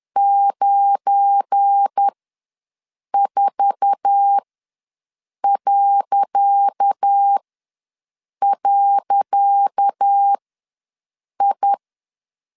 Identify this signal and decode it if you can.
Morse code, heard as a keyed tone, about 10 wpm, spaced as for 6 wpm, 800 Hz. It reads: 94..I
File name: LEZIONI / 10 SIGNS / 10C DOT ws 10wpm 4213